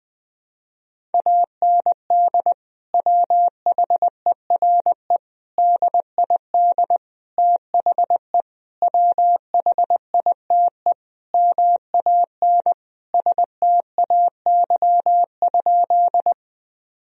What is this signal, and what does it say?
Morse code, heard as a keyed tone, about 20 wpm, 700 Hz. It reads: AND WHERE DID THE WHITE MAN STAY?
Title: Sentences from Top 1000 Words - Review Part 1 - 20wpm